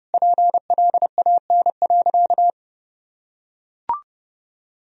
Morse code, heard as a keyed tone, about 30 wpm, 700 Hz. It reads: PLAN.
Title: Sentences from Top 1000 Words - Review Part 2 - 30wpm